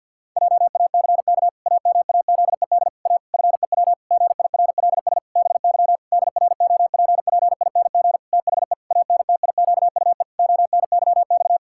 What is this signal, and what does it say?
Morse code, heard as a keyed tone, about 50 wpm, 700 Hz. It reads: JACK AMAZED A FEW GIRLS BY DROPPING THE ANTIQUE ONYX
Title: Pangram Challenge v2 - 50wpm